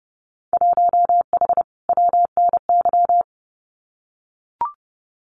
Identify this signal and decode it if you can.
Morse code, heard as a keyed tone, about 30 wpm, 700 Hz. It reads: 1H WNY
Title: Winter Field Day Exchanges 30wpm